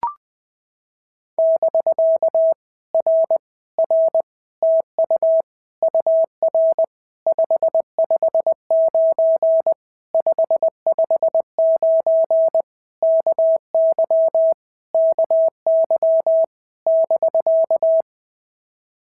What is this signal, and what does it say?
Morse code, heard as a keyed tone, about 20 wpm, 650 Hz. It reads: <BK> R R TU UR 559 559 KY KY <BK>
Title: POTA QSO Practice - 20wpm